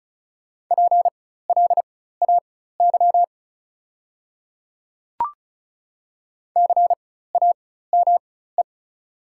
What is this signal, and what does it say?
Morse code, heard as a keyed tone, about 36 wpm, spaced as for 16 wpm, 700 Hz. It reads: PLAY CAME
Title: 4x Character Spacing - Top 300 Words - 35wpm